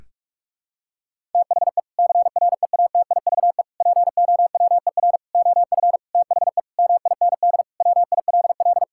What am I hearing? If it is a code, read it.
Morse code, heard as a keyed tone, about 46 wpm, 700 Hz. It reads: THE CREATIVE POWER OF THE MIND WILL
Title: Sentences from Top 1000 Words - Review Part 5 - 45wpm